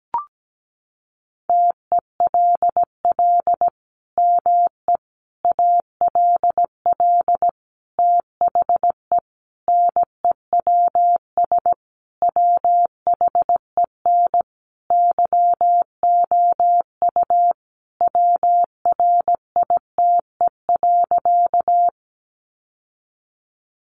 Morse code, heard as a keyed tone, about 17 wpm, 700 Hz. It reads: TELL ME ALL THE NEWS WHEN YOU WRITE.
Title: Sentences from Top 800 Words 17wpm